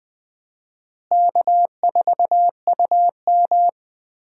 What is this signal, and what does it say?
Morse code, heard as a keyed tone, about 20 wpm, 700 Hz. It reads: K4UM
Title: USA Call Signs - 4 Characters - Repeated 3x - 20wpm